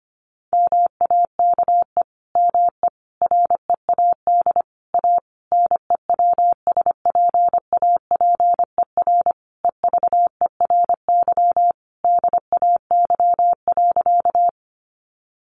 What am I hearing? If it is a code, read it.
Morse code, heard as a keyed tone, about 25 wpm, 700 Hz. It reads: MAKE ME READ A NEWSPAPER EVERY DAY.
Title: Sentences from Top 1100 Words 25wpm